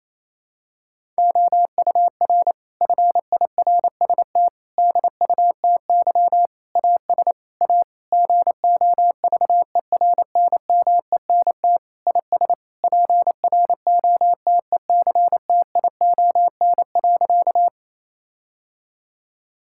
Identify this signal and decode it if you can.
Morse code, heard as a keyed tone, about 28 wpm, 700 Hz. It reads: OUR FIRST DUTY AS A GOVERNMENT IS PROTECTION.
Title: Sentences from Top 1600 Words 28wpm